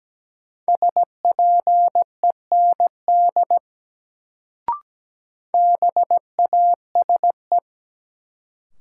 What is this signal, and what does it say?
Morse code, heard as a keyed tone, about 17 wpm, 700 Hz. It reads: SPEND BASE
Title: Sets of 5 Words - New Word Drill - 17wpm